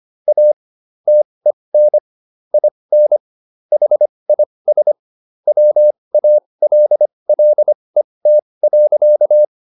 Morse code, heard as a keyed tone, about 25 wpm, spaced as for 20 wpm, 600 Hz. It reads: A TEN IN HIS WALLET.